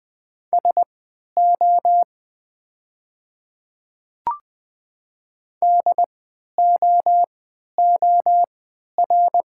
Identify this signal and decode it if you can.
Morse code, heard as a keyed tone, about 20 wpm, spaced as for 11 wpm, 700 Hz. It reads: SO DOOR
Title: 3x Character Spacing - Top 300 Words - 20wpm